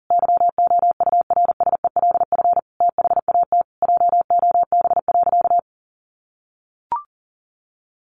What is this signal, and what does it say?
Morse code, heard as a keyed tone, about 40 wpm, 700 Hz. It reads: YOURSELF THAT JOB.
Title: Sentences from Top 1100 Words 40wpm